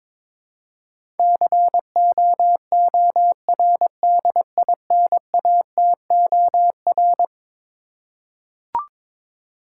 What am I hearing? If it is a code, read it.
Morse code, heard as a keyed tone, about 22 wpm, 700 Hz. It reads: COORDINATOR
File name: practice / 22wpm / Eleven Letter Words - 22wpm